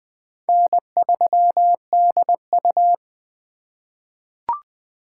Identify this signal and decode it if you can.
Morse code, heard as a keyed tone, about 20 wpm, 700 Hz. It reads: N3DU
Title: USA Call Signs - 4 Characters - 3x Character Spacing - 20wpm